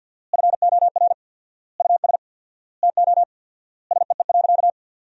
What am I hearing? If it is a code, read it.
Morse code, heard as a keyed tone, about 50 wpm, 700 Hz. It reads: FOR US TO SEE.
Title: 4x Word-Spacing - Sentences from Top 500 Words - 50wpm